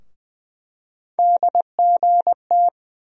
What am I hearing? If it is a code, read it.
Morse code, heard as a keyed tone, about 20 wpm, 700 Hz. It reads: DGT
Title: USA Call Sign Postfixes - 3 Letters 20wpm